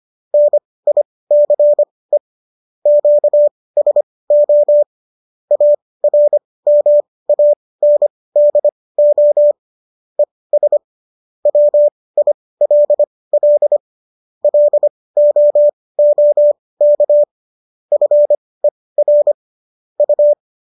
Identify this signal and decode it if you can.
Morse code, heard as a keyed tone, about 25 wpm, spaced as for 18 wpm, 590 Hz. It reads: NICE QSO ARMANDO ES WILL LOOK FER U